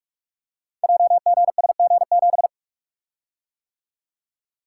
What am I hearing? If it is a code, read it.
Morse code, heard as a keyed tone, about 47 wpm, 700 Hz. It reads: JGSG7
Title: Code Groups - 5 Letter:Numbers - 45wpm